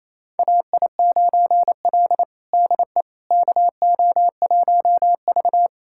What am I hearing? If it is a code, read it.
Morse code, heard as a keyed tone, about 28 wpm, 700 Hz. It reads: AI9L DE KO1V
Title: USA Callsign DE Callsign - 4 Characters - 28wpm